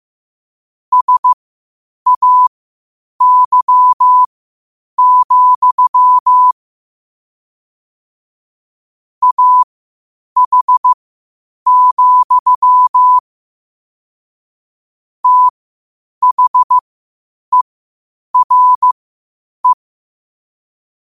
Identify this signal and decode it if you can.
Morse code, heard as a keyed tone, about 15 wpm, spaced as for 8 wpm, 1000 Hz. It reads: SAY, AH, THERE
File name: qotdfiles / 8wpm